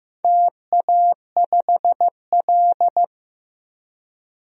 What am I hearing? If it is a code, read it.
Morse code, heard as a keyed tone, about 15 wpm, 700 Hz. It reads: TA5L
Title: DXCC-100 Call Signs - 4 Characters - 15wpm